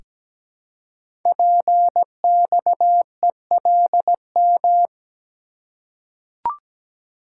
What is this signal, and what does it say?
Morse code, heard as a keyed tone, about 17 wpm, 700 Hz. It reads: PXELM
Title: Code Groups - 5 Letters - 17wpm